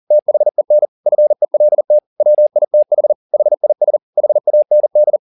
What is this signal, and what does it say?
Morse code, heard as a keyed tone, about 40 wpm, 600 Hz. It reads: THEN FELT WITH HIS HAND